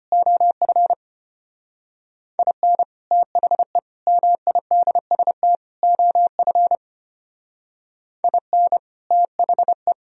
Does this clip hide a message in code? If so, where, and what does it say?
Yes: Morse code from 0.1 to 9.9 s, heard as a keyed tone, about 33 wpm, 700 Hz. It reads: OF IN THE MIDST OF IN THE